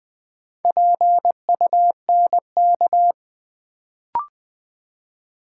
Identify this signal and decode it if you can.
Morse code, heard as a keyed tone, about 20 wpm, 700 Hz. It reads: PUNK
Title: Build A Word - 2 Letters At A Time - 4-Letter Words - 20wpm